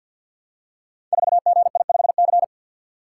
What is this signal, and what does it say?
Morse code, heard as a keyed tone, about 49 wpm, 700 Hz. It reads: 4GI5C
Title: Code Groups - 5 Letter:Numbers - 50wpm